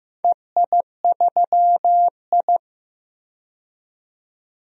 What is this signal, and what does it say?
Morse code, heard as a keyed tone, about 15 wpm, 700 Hz. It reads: EI3I
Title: DXCC-100 Call Signs - 4 Characters - 15wpm